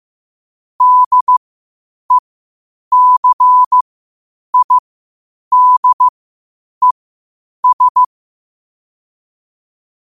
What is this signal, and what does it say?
Morse code, heard as a keyed tone, about 15 wpm, spaced as for 8 wpm, 1000 Hz. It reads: DECIDES
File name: qotdfiles / 8wpm